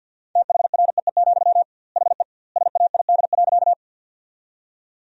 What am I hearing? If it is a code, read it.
Morse code, heard as a keyed tone, about 50 wpm, 700 Hz. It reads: THREE, HE SAID.
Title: 2x Word-Spacing - Sentences from Top 200 Words - 50wpm